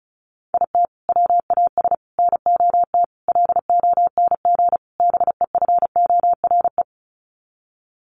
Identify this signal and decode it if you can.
Morse code, heard as a keyed tone, about 34 wpm, 700 Hz. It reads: IT WAS NOT LONG BEFORE
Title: Sets of 5 Words Encore2 35wpm